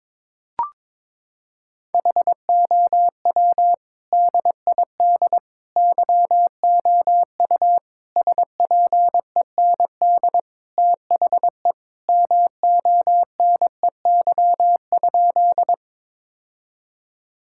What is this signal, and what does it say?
Morse code, heard as a keyed tone, about 22 wpm, 700 Hz. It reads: HOW DID YOU SPEND THE MONEY?
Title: Sentences from Top 1000 Words - Review Part 5 - 22wpm